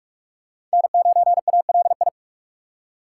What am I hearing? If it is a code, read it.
Morse code, heard as a keyed tone, about 42 wpm, 700 Hz. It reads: N9ALI
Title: USA Call Signs - 5 Characters - 6x Character Spacing - 45wpm